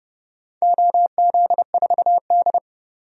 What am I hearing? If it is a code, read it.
Morse code, heard as a keyed tone, about 32 wpm, 700 Hz. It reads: OZ4D
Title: DXCC-100 Call Signs - 4 Characters - 30wpm